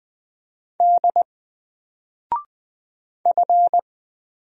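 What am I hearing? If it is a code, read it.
Morse code, heard as a keyed tone, about 20 wpm, 700 Hz. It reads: D F
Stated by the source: Single letter and number Morse drill at 20 wpm, from N4LCD.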